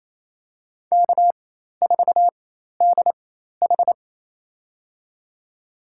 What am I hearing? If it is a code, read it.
Morse code, heard as a keyed tone, about 28 wpm, spaced as for 13 wpm, 700 Hz. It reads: K4DH
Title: USA Call Signs - 4 Characters - 4x Character Spacing - 28wpm